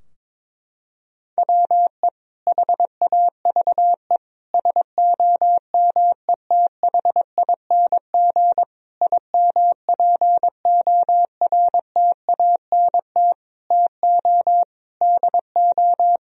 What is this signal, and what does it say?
Morse code, heard as a keyed tone, about 22 wpm, 700 Hz. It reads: WE HAVE SOMETHING IMPORTANT TO DO
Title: Sentences from Top 1000 Words - Review Part 4 - 22wpm